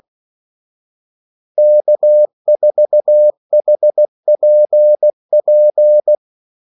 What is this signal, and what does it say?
Morse code, heard as a keyed tone, about 16 wpm, 600 Hz. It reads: K4HPP
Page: Kaldesignaler 16 wpm